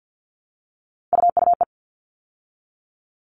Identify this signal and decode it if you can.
Morse code, heard as a keyed tone, about 50 wpm, 700 Hz. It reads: UUE
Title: All Cut Numbers - 3-Digit 50wpm